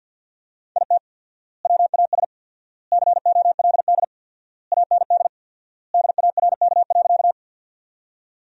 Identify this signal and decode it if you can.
Morse code, heard as a keyed tone, about 48 wpm, 700 Hz. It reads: IT WAS COLD AND DARK.
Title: 4x Word-Spacing - Sentences from Top 500 Words - 50wpm